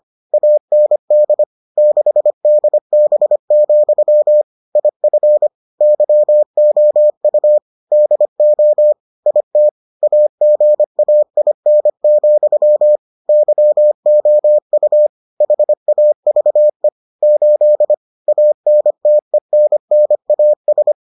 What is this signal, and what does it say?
Morse code, heard as a keyed tone, about 25 wpm, 600 Hz. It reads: AND 6DB, IF YOU DO IT AGAIN, YOU HAVE 8 ANTENNAS